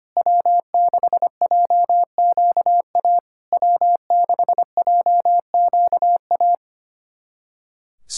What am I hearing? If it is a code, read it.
Morse code, heard as a keyed tone, about 25 wpm, 690 Hz. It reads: W6JQA W6JQA
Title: SOTA QSO Practice 25wpm